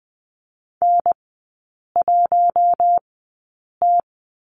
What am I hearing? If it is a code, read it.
Morse code, heard as a keyed tone, about 20 wpm, spaced as for 8 wpm, 700 Hz. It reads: N1T
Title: 041 Lesson 8e - Number 1 - Three Characters 20wpm